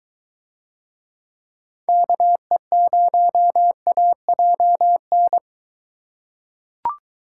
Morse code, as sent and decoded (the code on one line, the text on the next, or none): -.- . ----- .- .--- -.
KE0AJN